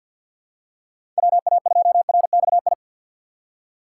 Morse code, heard as a keyed tone, about 53 wpm, 700 Hz. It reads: WA2RXI
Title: USA Call Signs - 6 Characters - 4x Character Spacing - 50wpm